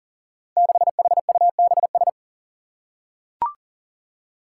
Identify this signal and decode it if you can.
Morse code, heard as a keyed tone, about 39 wpm, 700 Hz. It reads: 6HUBS
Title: Five Challenging Letter-Number Combinations 40wpm